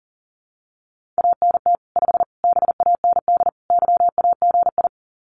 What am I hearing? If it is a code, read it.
Morse code, heard as a keyed tone, about 40 wpm, 700 Hz. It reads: ANT 5 BAND YAGI
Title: Common QSO Phrases 40wpm